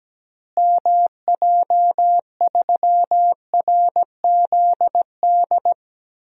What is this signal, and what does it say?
Morse code, heard as a keyed tone, about 17 wpm, 700 Hz. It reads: MJ3RZD
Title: DXCC-100 Call Signs - 6 Characters - Repeated 3x - 17wpm